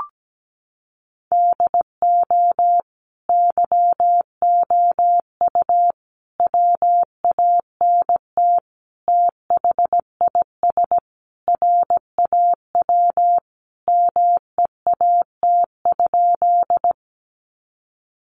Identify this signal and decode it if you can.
Morse code, heard as a keyed tone, about 17 wpm, 700 Hz. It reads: DO YOU WANT THIS RAW MEAT?
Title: Sentences from Top 1500 Words 17wpm